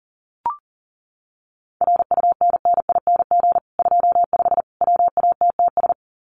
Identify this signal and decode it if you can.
Morse code, heard as a keyed tone, about 40 wpm, 700 Hz. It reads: RUNNING 25 WATTS
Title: Common QSO Phrases 40wpm